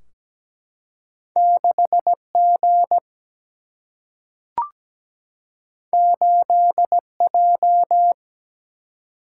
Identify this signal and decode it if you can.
Morse code, heard as a keyed tone, about 17 wpm, 700 Hz. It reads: 6 G 8 J
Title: Single-Letter Single-Number 17wpm